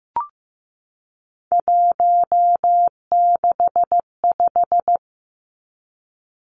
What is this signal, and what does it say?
Morse code, heard as a keyed tone, about 15 wpm, 700 Hz. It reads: 165
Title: Three Challenging Letter-Number Combinations 15wpm